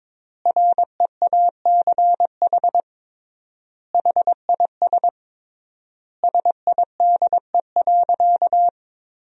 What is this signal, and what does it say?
Morse code, heard as a keyed tone, about 22 wpm, 700 Hz. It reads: REACH HIS SIDE.
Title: 3x Word-Spacing - Sentences from Top 400 Words - 22wpm